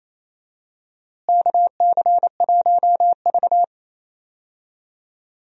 Morse code, heard as a keyed tone, about 28 wpm, 700 Hz. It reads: KC1V
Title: USA Call Signs - 4 Characters - Repeated 2x - 28wpm